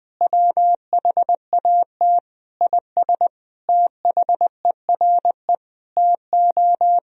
Morse code, heard as a keyed tone, about 20 wpm, 700 Hz. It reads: WHAT IS THERE TO
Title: Sentences from Top 200 Words and Abbreviations 20wpm